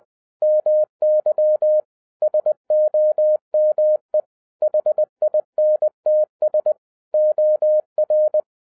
Morse code, heard as a keyed tone, about 20 wpm, 610 Hz. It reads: MY SOME HINTS OR